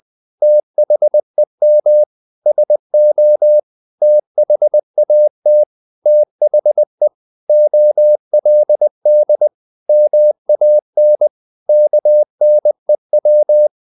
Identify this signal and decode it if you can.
Morse code, heard as a keyed tone, about 20 wpm, 600 Hz. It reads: THEM SO THAT THE OLD MAN KNEW
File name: the old man and the sea part 2 20wpm